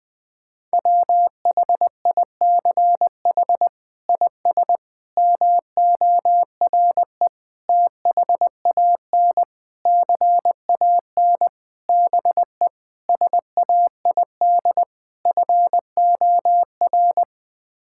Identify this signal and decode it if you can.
Morse code, heard as a keyed tone, about 21 wpm, 700 Hz. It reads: WHICH IS MORE THAN CAN BE SAID FOR
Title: Sets of 8 Words - 20wpm